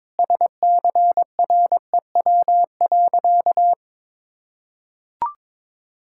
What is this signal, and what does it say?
Morse code, heard as a keyed tone, about 22 wpm, 700 Hz. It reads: SCREW.
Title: Sentences from Top 1600 Words 22wpm